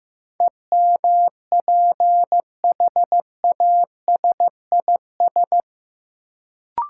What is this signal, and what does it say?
Morse code, heard as a keyed tone, about 15 wpm, 700 Hz. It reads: EMPHASIS
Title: Sentences from Top 1400 Words - New Word Drill - 15wpm